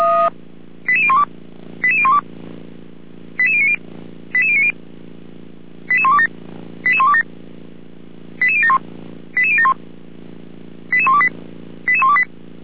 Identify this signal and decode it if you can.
Five-tone selective call (ZVEI), heard as quick runs of five tones, tones of about 70 ms. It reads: 88012 88012 88090 88090 88127 88127 88071 88071 88128 88128
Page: /455/SDR/ZVEI/